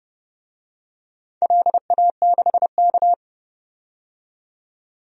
Morse code, heard as a keyed tone, about 31 wpm, 700 Hz. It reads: LA6K
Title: DXCC-100 Call Signs - 4 Characters - 30wpm